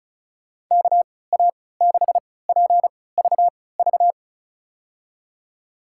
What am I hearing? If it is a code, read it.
Morse code, heard as a keyed tone, about 36 wpm, spaced as for 20 wpm, 700 Hz. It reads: KA6PVV
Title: USA Call Signs - 3x Character Spacing - 35wpm